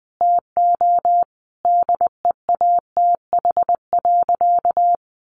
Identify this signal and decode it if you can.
Morse code, heard as a keyed tone, about 20 wpm, 700 Hz. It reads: TO DEATH.
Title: Sentences from Top 1400 Words 20wpm